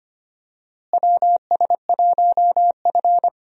Morse code, heard as a keyed tone, about 25 wpm, 700 Hz. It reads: WS1F
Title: USA Call Signs - 4 Characters - Repeated 2x - 25wpm